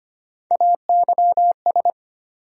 Morse code, heard as a keyed tone, about 25 wpm, 700 Hz. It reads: AYS
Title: USA Call Sign Postfixes - 3 Letters 25wpm